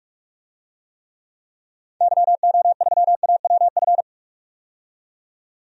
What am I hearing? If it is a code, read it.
Morse code, heard as a keyed tone, about 46 wpm, 700 Hz. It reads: YO3AWF
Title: DXCC-100 Call Signs - 6 Characters - 45wpm